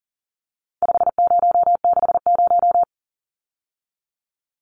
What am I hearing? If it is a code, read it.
Morse code, heard as a keyed tone, about 40 wpm, 700 Hz. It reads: 5060
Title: Numbers - 4-Digit 40wpm